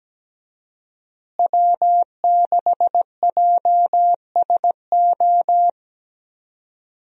Morse code, heard as a keyed tone, about 17 wpm, 700 Hz. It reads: W6JSO
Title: USA Call Signs - 5 Characters - Repeated 3x - 17wpm